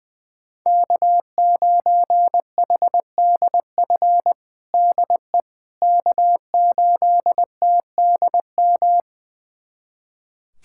Morse code, heard as a keyed tone, about 21 wpm, 700 Hz. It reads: K9HDF DE K8TDM
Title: USA Callsign DE Callsign - 5 Characters - 20wpm